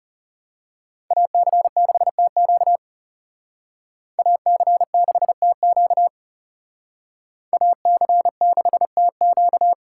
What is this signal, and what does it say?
Morse code, heard as a keyed tone, about 35 wpm, 690 Hz. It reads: AC6TQ AC6TQ AC6TQ
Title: Speed Racing - USA Call Signs 30wpm